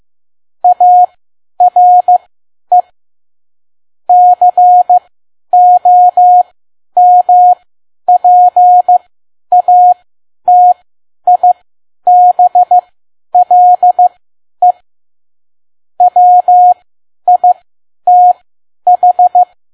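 Morse code, heard as a keyed tone, about 15 wpm, spaced as for 10 wpm, 720 Hz. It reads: ARE COMPATIBLE WITH
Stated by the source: Morse code practice at 10 WPM.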